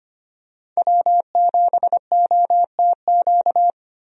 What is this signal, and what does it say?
Morse code, heard as a keyed tone, about 25 wpm, 690 Hz. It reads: W7OTQ
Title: USA Call Signs - 5 Characters - Repeated 2x - 25wpm